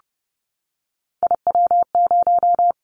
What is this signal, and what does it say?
Morse code, heard as a keyed tone, about 30 wpm, 700 Hz. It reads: IW0
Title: Callsign Prefixes for Top 25 DXCC - Repeated 3x - 30wpm